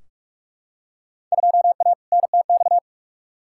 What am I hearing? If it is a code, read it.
Morse code, heard as a keyed tone, about 46 wpm, 700 Hz. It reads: 2A NTX
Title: Field Day Exchanges - 45wpm